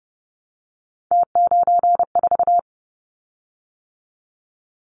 Morse code, heard as a keyed tone, about 30 wpm, 700 Hz. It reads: T94
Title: Callsign Prefixes for Top 50 DXCC - 30wpm